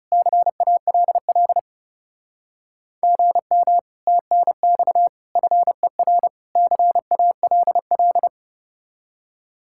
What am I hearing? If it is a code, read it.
Morse code, heard as a keyed tone, about 40 wpm, 700 Hz. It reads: CALL GM TNX FER CALL